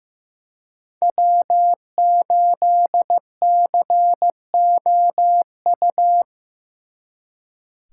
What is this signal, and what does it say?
Morse code, heard as a keyed tone, about 18 wpm, 700 Hz. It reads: W8COU
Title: Speed Racing - USA 1x3 Call Signs 15wpm